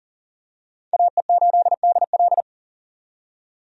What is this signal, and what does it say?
Morse code, heard as a keyed tone, about 38 wpm, 700 Hz. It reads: AE8DL
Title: USA Call Signs - 5 Characters - 40wpm